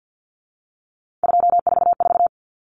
Morse code, heard as a keyed tone, about 50 wpm, 700 Hz. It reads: 244
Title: Numbers - 3-Digit 50wpm